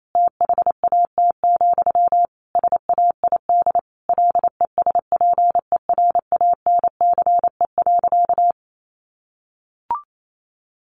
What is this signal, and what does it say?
Morse code, heard as a keyed tone, about 28 wpm, 700 Hz. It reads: THAT, SAID LESPERANCE.
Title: A Sound Of Thunder 28wpm